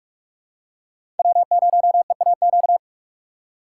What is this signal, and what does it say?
Morse code, heard as a keyed tone, about 44 wpm, 700 Hz. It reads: W0EAQ